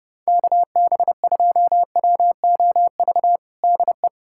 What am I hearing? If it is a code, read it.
Morse code, heard as a keyed tone, about 32 wpm, 700 Hz. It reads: KB2WOV DE
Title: USA Callsign DE Callsign - 6 Characters - 30wpm